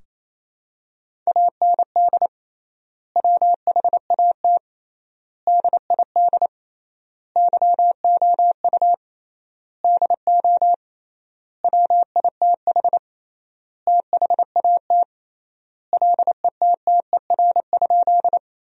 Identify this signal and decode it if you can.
Morse code, heard as a keyed tone, about 28 wpm, 700 Hz. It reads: AND WHAT DID YOU DO WITH THAT LETTER?
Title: 3x Word-Spacing - Sentences from Top 300 Words - 28wpm